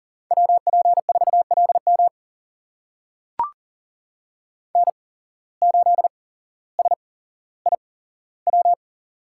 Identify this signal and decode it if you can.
Morse code, heard as a keyed tone, about 40 wpm, spaced as for 17 wpm, 700 Hz. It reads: WP4LM N8SIW
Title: USA Call Signs - 5 Characters - 8x Character Spacing - 40wpm